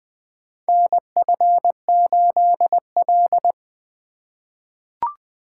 Morse code, heard as a keyed tone, about 20 wpm, 700 Hz. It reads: NF8L
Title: USA Call Signs - 4 Characters - 3x Character Spacing - 20wpm